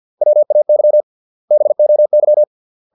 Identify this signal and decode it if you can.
Morse code, heard as a keyed tone, about 50 wpm, 600 Hz. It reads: WAY BOY
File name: 632 words / 632 Words 50WPM